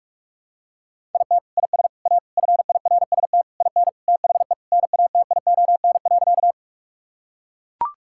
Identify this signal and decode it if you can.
Morse code, heard as a keyed tone, about 43 wpm, 700 Hz. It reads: IT IS A FIRST IN THE NATION.
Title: Sentences from Top 1000 Words - Review Part 6 - 45wpm